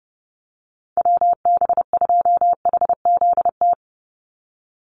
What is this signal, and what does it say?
Morse code, heard as a keyed tone, about 30 wpm, 700 Hz. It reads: WB2HZT
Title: USA Call Signs 30wpm